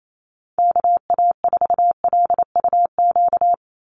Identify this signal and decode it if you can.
Morse code, heard as a keyed tone, about 28 wpm, 700 Hz. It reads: KA4LUQ
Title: USA Call Signs 28wpm